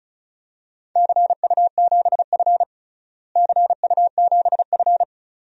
Morse code, heard as a keyed tone, about 36 wpm, 700 Hz. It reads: CU7F CU7F
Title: DXCC-100 Call Signs - 4 Characters - Repeated 2x - 35wpm